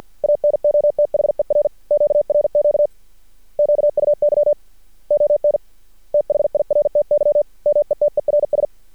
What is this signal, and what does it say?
Morse code, heard as a keyed tone, about 48 wpm, 590 Hz. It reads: ANOTHER QRQ QUY ON THIRTY METERS
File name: archived / CRSnet